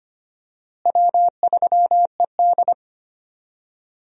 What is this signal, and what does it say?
Morse code, heard as a keyed tone, about 25 wpm, 700 Hz. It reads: W3ED
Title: USA Call Signs - 4 Characters - Repeated 2x - 25wpm